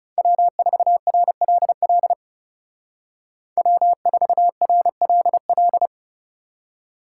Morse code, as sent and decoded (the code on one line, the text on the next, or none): .-- ....- .-. .-.. .-.. / .-- ....- .-. .-.. .-..
W4RLL W4RLL